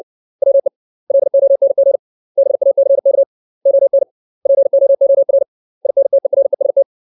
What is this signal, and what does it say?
Morse code, heard as a keyed tone, about 60 wpm, 550 Hz. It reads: WE LONG BACK ON POOR INTEREST
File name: 632 words / 632 Words 60WPM